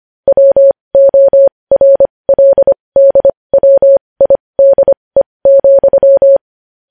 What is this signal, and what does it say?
Morse code, heard as a keyed tone, about 25 wpm, spaced as for 20 wpm, 550 Hz. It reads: WORLDWIDE,
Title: News Headlines at 20